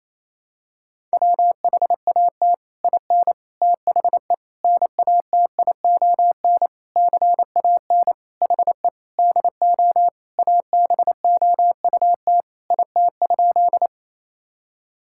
Morse code, heard as a keyed tone, about 28 wpm, 700 Hz. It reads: WHAT IN THE NATION CAN HE DO ABOUT IT?
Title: Sentences from Top 1000 Words - Review Part 6 - 28wpm